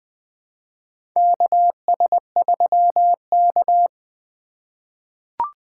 Morse code, heard as a keyed tone, about 21 wpm, 700 Hz. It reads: KS3K
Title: USA Call Signs - 4 Characters - 4x Character Spacing - 20wpm